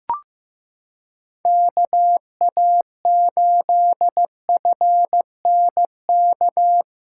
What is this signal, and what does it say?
Morse code, heard as a keyed tone, about 15 wpm, 700 Hz. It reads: KA8FNK
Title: USA Callsign DE Callsign - 6 Characters - 15wpm